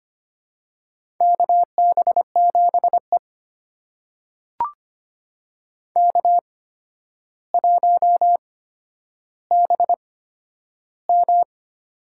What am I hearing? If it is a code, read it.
Morse code, heard as a keyed tone, about 25 wpm, spaced as for 11 wpm, 700 Hz. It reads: KB7E K1BM
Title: USA Call Signs - 8x Character Spacing - 25wpm